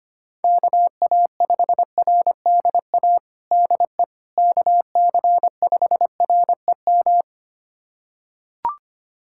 Morse code, heard as a keyed tone, about 25 wpm, 700 Hz. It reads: KA5RDA DE KC5REM
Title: USA Callsign DE Callsign 25wpm